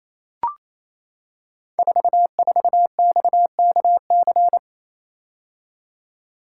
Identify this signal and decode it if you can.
Morse code, heard as a keyed tone, about 28 wpm, 700 Hz. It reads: 44XKC
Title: Code Groups - 5 Letter:Numbers - 28wpm